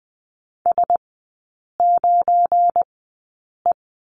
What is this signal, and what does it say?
Morse code, heard as a keyed tone, about 20 wpm, spaced as for 8 wpm, 700 Hz. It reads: S9E
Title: 172 Lesson 28e - Number 9 - Three Characters 20wpm